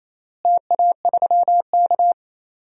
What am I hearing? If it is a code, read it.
Morse code, heard as a keyed tone, about 28 wpm, 700 Hz. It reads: TA3K
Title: DXCC-100 Call Signs - 4 Characters - Repeated 3x - 28wpm